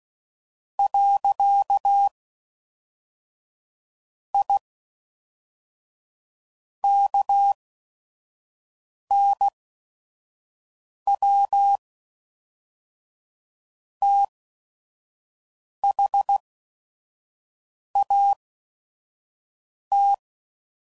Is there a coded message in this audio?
Yes: Morse code from 0.8 to 20.1 s, heard as a keyed tone, about 16 wpm, 800 Hz. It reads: . I K N W T H A T